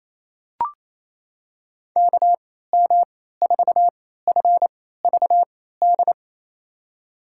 Morse code, heard as a keyed tone, about 28 wpm, spaced as for 16 wpm, 700 Hz. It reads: KM4FVD
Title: USA Call Signs - 6 Characters - 3x Character Spacing - 28wpm